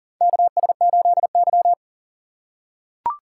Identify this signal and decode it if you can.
Morse code, heard as a keyed tone, about 40 wpm, 700 Hz. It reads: KS8Y
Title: USA Call Signs - 4 Characters - 4x Character Spacing - 40wpm